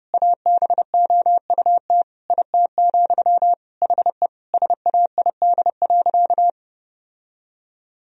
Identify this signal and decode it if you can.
Morse code, heard as a keyed tone, about 32 wpm, 700 Hz. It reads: ABOUT IT, HE SAID.
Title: Sentences from Top 1000 Words - Review Part 3 - 30wpm